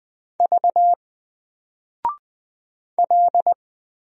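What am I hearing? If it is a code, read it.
Morse code, heard as a keyed tone, about 20 wpm, 700 Hz. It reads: V L
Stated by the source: Single letter and number Morse drill at 20 wpm, from N4LCD.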